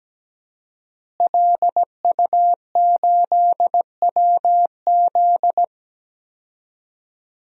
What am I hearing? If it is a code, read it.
Morse code, heard as a keyed tone, about 17 wpm, 700 Hz. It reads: LU8WZ
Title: DXCC-100 Call Signs - 5 Characters - Repeated 3x - 17wpm